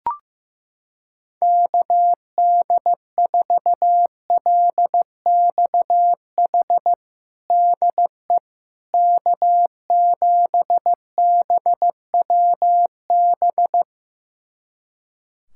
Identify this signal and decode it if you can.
Morse code, heard as a keyed tone, about 15 wpm, 700 Hz. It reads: KD4LXH DE K7BWB
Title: USA Callsign DE Callsign 15wpm